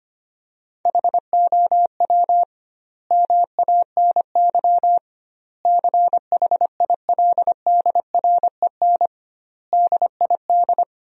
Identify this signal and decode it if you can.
Morse code, heard as a keyed tone, about 25 wpm, 700 Hz. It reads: HOW MANY CHILDREN DID
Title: 2x Word-Spacing - Sentences from Top 400 Words - 25wpm